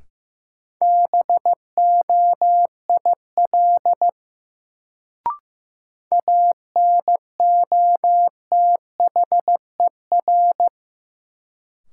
Morse code, heard as a keyed tone, about 16 wpm, 700 Hz. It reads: BOIL ANOTHER
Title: Sets of 5 Words Encore - New Word Drill - 15wpm